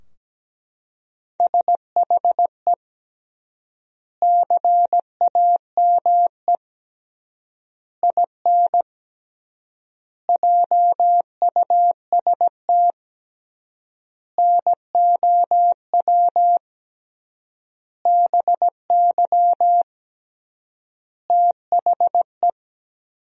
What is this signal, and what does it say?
Morse code, heard as a keyed tone, about 17 wpm, 700 Hz. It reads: SHE CAME IN JUST NOW BY THE
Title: 3x Word-Spacing - Sentences from Top 300 Words - 17wpm